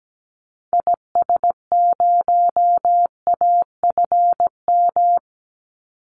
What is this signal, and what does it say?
Morse code, heard as a keyed tone, about 17 wpm, 700 Hz. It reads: IS0AFM
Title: Call Signs - 17wpm